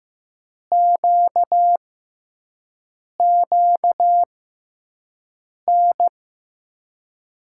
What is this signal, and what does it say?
Morse code, heard as a keyed tone, about 15 wpm, spaced as for 5 wpm, 700 Hz. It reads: QQN
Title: USA Call Signs - 6 Characters - 6x Character Spacing - 15wpm